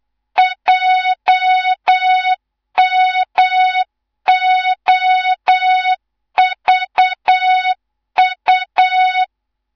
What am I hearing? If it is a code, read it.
Morse code, heard as a keyed tone, about 8 wpm, 3700 Hz, with noise. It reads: JMOVU
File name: Morze-HA2MN / hang / 18 IRASJELEK2